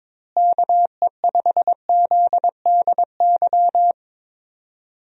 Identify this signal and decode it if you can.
Morse code, heard as a keyed tone, about 22 wpm, 700 Hz. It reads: KE5ZDY
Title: USA Call Signs - 6 Characters - Repeated 3x - 22wpm